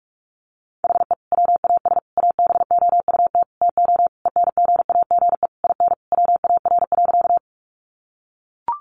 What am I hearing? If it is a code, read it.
Morse code, heard as a keyed tone, about 44 wpm, 700 Hz. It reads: HE WAS ABOUT TO ENGAGE IN WAR.
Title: Sentences from Top 1500 Words 45wpm